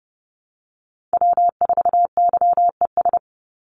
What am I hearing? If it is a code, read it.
Morse code, heard as a keyed tone, about 30 wpm, 700 Hz. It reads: W4YES
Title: Call Signs Repeated 3x - 30wpm